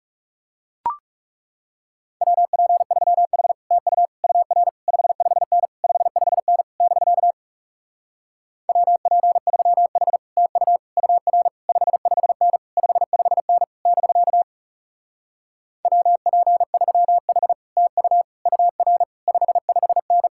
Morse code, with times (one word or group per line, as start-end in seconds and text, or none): 2.2-3.5 WP3H
3.7-4.1 TU
4.2-4.7 UR
4.9-5.7 55N
5.8-6.6 55N
6.8-7.3 <BK>
8.7-10.2 WP3H
10.4-10.8 TU
11.0-11.5 UR
11.7-12.6 55N
12.8-13.6 55N
13.9-14.4 <BK>
15.8-17.5 WP3H
17.8-18.2 TU
18.5-19.0 UR
19.3-20.3 55N